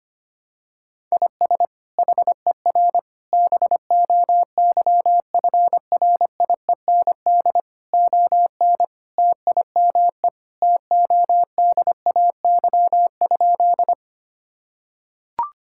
Morse code, heard as a keyed tone, about 25 wpm, 700 Hz. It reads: IS HER BOYFRIEND ON TIME TODAY?
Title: Sentences from Top 1000 Words - Review Part 5 - 25wpm